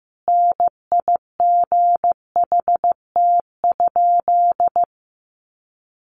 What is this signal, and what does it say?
Morse code, heard as a keyed tone, about 15 wpm, 700 Hz. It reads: NIGHT?
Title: Sentences from Top 300 Words 15wpm